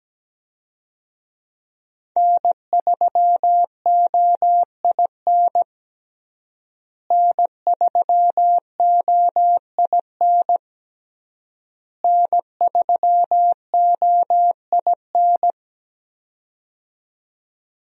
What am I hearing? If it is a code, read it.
Morse code, heard as a keyed tone, about 17 wpm, 700 Hz. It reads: N3OIN N3OIN N3OIN
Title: USA Call Signs - 5 Characters - Repeated 3x - 17wpm